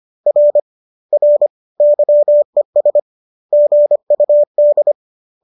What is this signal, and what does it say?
Morse code, heard as a keyed tone, about 25 wpm, 570 Hz. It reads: R R YES GUD
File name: qso218 25